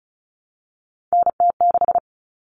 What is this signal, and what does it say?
Morse code, heard as a keyed tone, about 34 wpm, 700 Hz. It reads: NT6
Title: All Cut Numbers - 3-Digit 35wpm